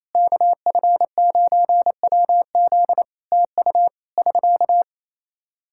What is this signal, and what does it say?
Morse code, heard as a keyed tone, about 28 wpm, 700 Hz. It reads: KF9WZ TU <SK>
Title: Practice QSO - Exchange 4 - Ending - 28wpm